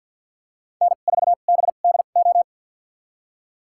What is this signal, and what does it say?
Morse code, heard as a keyed tone, about 53 wpm, spaced as for 37 wpm, 700 Hz. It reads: N4BDO
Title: USA Call Signs - 5 Characters - 2x Character Spacing - 50wpm